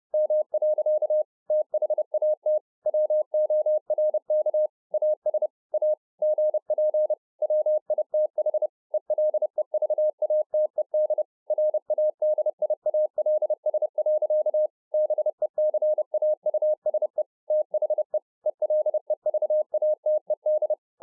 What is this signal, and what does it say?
Morse code, heard as a keyed tone, about 30 wpm, 610 Hz. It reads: M. THAT WORK AS A GP WITH ELEVATED RADIALS. BECAUSE THE ELEVATED